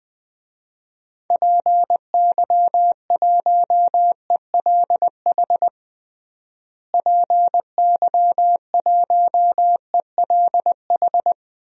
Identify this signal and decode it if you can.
Morse code, heard as a keyed tone, about 20 wpm, 700 Hz. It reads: PY1ELH PY1ELH
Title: DXCC-100 Call Signs - 6 Characters - Repeated 3x - 20wpm